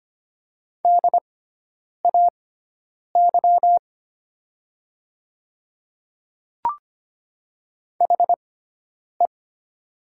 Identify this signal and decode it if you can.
Morse code, heard as a keyed tone, about 25 wpm, spaced as for 9 wpm, 700 Hz. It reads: DAY HE
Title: 6x Character Spacing - Top 100 Words - 25wpm